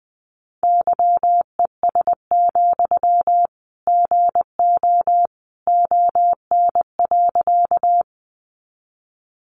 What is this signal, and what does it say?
Morse code, heard as a keyed tone, about 20 wpm, 700 Hz. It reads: YES, GO ON.
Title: Sentences from Top 500 Words 20wpm